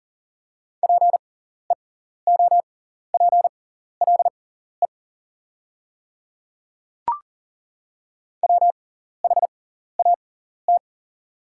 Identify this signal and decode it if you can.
Morse code, heard as a keyed tone, about 38 wpm, spaced as for 14 wpm, 700 Hz. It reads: PEOPLE WHAT
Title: 6x Character Spacing - Top 100 Words - 40wpm